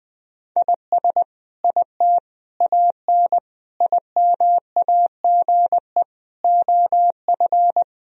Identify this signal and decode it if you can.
Morse code, heard as a keyed tone, about 21 wpm, 700 Hz. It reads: IS IT AN IMAGE OF
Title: Sentences from Top 1000 Words - Review Part 4 - 20wpm